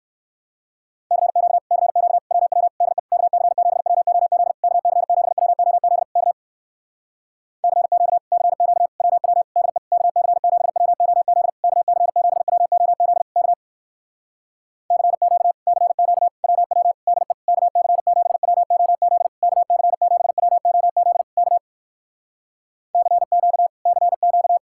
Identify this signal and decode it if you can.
Morse code, heard as a keyed tone, about 60 wpm, 700 Hz. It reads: CQ CQ WW DE KC7WOZ KC7WOZ K CQ CQ WW DE KC7WOZ KC7WOZ K CQ CQ WW DE KC7WOZ KC7WOZ K CQ CQ